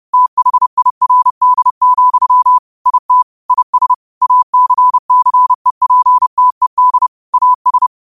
Morse code, heard as a keyed tone, about 30 wpm, 1000 Hz. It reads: THIRD, IT IS ACCEPTED AS
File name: qotdfiles / 30wpm